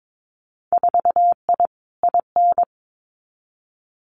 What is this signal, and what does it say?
Morse code, heard as a keyed tone, about 22 wpm, 700 Hz. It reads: 4I IN
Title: Winter Field Day Exchanges Encore 22wpm